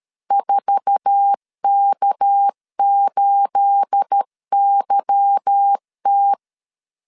Morse code, heard as a keyed tone, about 13 wpm, 800 Hz. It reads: 4K8YT
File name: LEZIONI / 9 28 12wpm 2351